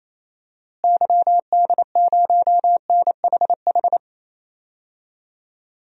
Morse code, heard as a keyed tone, about 28 wpm, 700 Hz. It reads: YD0NHH
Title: DXCC-100 Call Signs - 6 Characters - Repeated 2x - 28wpm